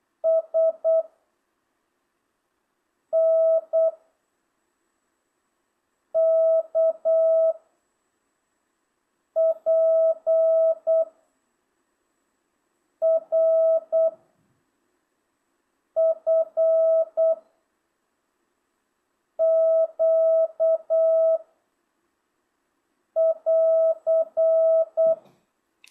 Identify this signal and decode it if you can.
Morse code, heard as a keyed tone, about 8 wpm, 660 Hz. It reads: S N K P R F Q <AR>